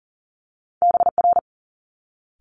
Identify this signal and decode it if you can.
Morse code, heard as a keyed tone, about 38 wpm, 700 Hz. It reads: BR
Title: Multi-letter Phonemes Quick Fire 40wpm